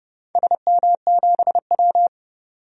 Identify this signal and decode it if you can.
Morse code, heard as a keyed tone, about 31 wpm, 700 Hz. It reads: SM7W
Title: DXCC-100 Call Signs - 4 Characters - 30wpm